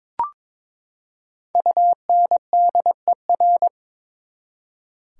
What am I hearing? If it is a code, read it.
Morse code, heard as a keyed tone, about 22 wpm, 700 Hz. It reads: UNDER
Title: Prefixes and Suffixes 22wpm